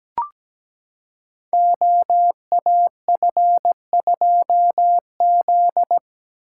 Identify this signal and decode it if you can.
Morse code, heard as a keyed tone, about 17 wpm, 700 Hz. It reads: OAF2Z
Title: Code Groups - 5 Letter:Numbers - 17wpm